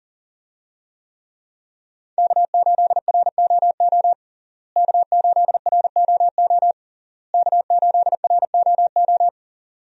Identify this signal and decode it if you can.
Morse code, heard as a keyed tone, about 40 wpm, 700 Hz. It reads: K8ROO K8ROO K8ROO
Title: USA Call Signs - 5 Characters - Repeated 3x - 40wpm